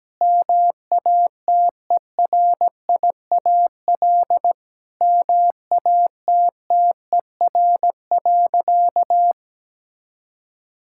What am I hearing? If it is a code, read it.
Morse code, heard as a keyed tone, about 17 wpm, 700 Hz. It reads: MATERIAL MATTER.
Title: Sentences from Top 1000 Words - Review Part 5 - 17wpm